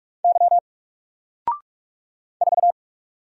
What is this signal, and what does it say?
Morse code, heard as a keyed tone, about 45 wpm, 700 Hz. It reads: Y 4
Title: Single Letter-Number 45wpm